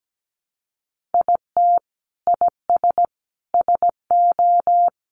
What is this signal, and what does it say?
Morse code, heard as a keyed tone, about 17 wpm, 700 Hz. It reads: IT IS SO
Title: Sentences from Top 300 Words 17wpm